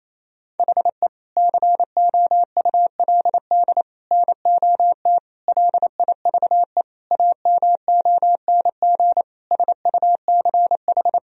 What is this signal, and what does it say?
Morse code, heard as a keyed tone, about 28 wpm, 700 Hz. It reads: HE COULD NOT LIVE AMONG SUCH
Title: Sentences from Top 1000 Words - Review Part 3 - 28wpm